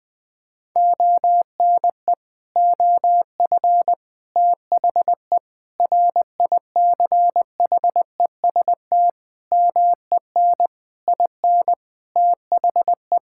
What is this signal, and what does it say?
Morse code, heard as a keyed tone, about 21 wpm, 700 Hz. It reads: ONE OF THE RICHEST MEN IN THE
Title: Sets of 8 Words - 20wpm